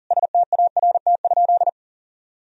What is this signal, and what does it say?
Morse code, heard as a keyed tone, about 40 wpm, 700 Hz. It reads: START?
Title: Sentences from Top 1000 Words - Review Part 1 - 40wpm